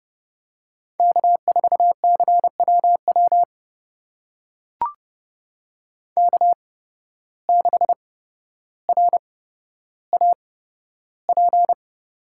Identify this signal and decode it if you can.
Morse code, heard as a keyed tone, about 30 wpm, spaced as for 13 wpm, 700 Hz. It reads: K4CWW K6RAP
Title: USA Call Signs - 5 Characters - 8x Character Spacing - 30wpm